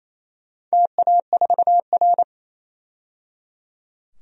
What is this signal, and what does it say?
Morse code, heard as a keyed tone, about 28 wpm, 700 Hz. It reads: TA4R
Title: DXCC-100 Call Signs - 4 Characters - 28wpm